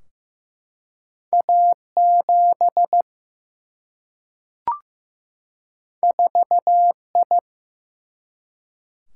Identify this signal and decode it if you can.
Morse code, heard as a keyed tone, about 15 wpm, 700 Hz. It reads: A7 4I
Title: Single-Letter Single-Number 15wpm